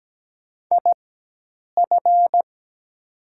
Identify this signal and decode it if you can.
Morse code, heard as a keyed tone, about 17 wpm, spaced as for 8 wpm, 700 Hz. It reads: IF
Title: 4x Character Spacing - Top 200 Words - 17wpm